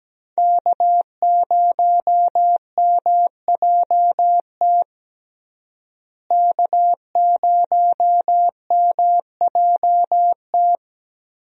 Morse code, heard as a keyed tone, about 17 wpm, 700 Hz. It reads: K0MJT K0MJT
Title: USA Call Signs - 5 Characters - Repeated 2x - 17wpm